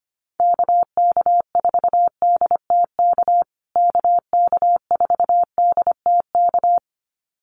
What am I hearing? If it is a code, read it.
Morse code, heard as a keyed tone, about 25 wpm, 700 Hz. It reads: KK4DTK KK4DTK
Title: USA Call Signs Repeated 2x 25wpm